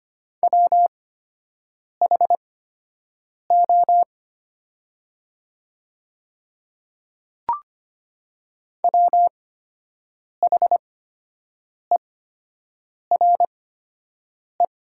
Morse code, heard as a keyed tone, about 25 wpm, spaced as for 7 wpm, 700 Hz. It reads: WHO WHERE
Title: 8x Character Spacing - Top 300 Words - 25wpm